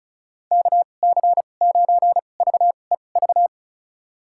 Morse code, heard as a keyed tone, about 36 wpm, spaced as for 25 wpm, 700 Hz. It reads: KC9VEV
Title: USA Call Signs - 6 Characters - 2x Character Spacing - 35wpm